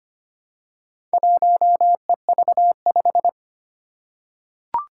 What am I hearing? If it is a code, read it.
Morse code, heard as a keyed tone, about 25 wpm, 700 Hz. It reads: 1EV5
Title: Four Challenging Letter-Number Combinations 25wpm